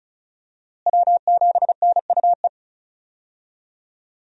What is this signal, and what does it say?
Morse code, heard as a keyed tone, about 34 wpm, 700 Hz. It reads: W7NUE
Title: USA Call Signs - 5 Characters - Repeated 2x - 35wpm